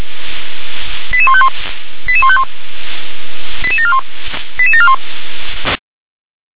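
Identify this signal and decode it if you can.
Five-tone selective call (ZVEI), heard as quick runs of five tones, tones of about 70 ms. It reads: 88151 88151 88741 88741
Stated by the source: Five-tone ZVEI selective call 88081 received by SDR.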